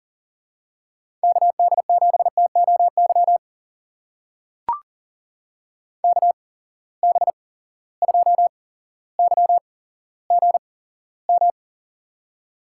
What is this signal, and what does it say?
Morse code, heard as a keyed tone, about 40 wpm, spaced as for 17 wpm, 700 Hz. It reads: KD7TOY KB2YGM
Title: USA Call Signs - 6 Characters - 8x Character Spacing - 40wpm